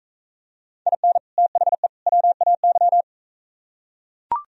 Morse code, heard as a keyed tone, about 42 wpm, 700 Hz. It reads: IN THE WAY